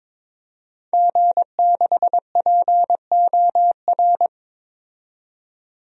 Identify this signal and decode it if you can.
Morse code, heard as a keyed tone, about 22 wpm, 700 Hz. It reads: G6POR
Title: DXCC-100 Call Signs - 5 Characters - 22wpm